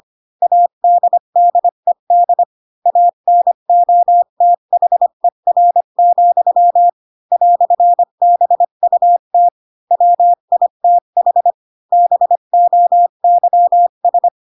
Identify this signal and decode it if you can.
Morse code, heard as a keyed tone, about 25 wpm, 700 Hz. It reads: ADDED ANOTHER, "BUT WITH BOYS
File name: Pinocchio / Pinocchio ch3 20 25 ik1hge